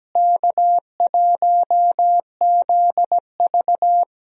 Morse code, heard as a keyed tone, about 17 wpm, 700 Hz. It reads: K1ZV
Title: USA Call Signs - 4 Characters - Repeated 2x - 17wpm